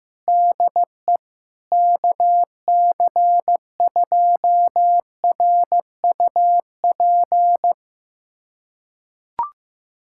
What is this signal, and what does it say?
Morse code, heard as a keyed tone, about 15 wpm, 700 Hz. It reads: DE KC2RUP
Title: USA Callsign DE Callsign - 6 Characters - 15wpm